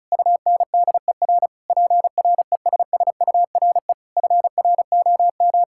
Morse code, heard as a keyed tone, about 36 wpm, 700 Hz. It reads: UNDER PRESSURE FROM
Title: Sentences from Top 1000 Words - Review Part 6 - 35wpm